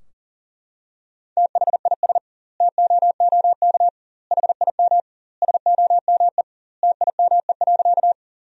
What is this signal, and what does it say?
Morse code, heard as a keyed tone, about 40 wpm, 700 Hz. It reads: THIS TOOK HIM SOME TIME.
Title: 2x Word-Spacing - Sentences from Top 300 Words - 40wpm